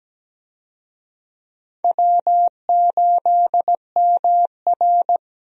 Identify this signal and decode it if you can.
Morse code, heard as a keyed tone, about 17 wpm, 700 Hz. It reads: W8MR
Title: USA Call Signs - 4 Characters - 17wpm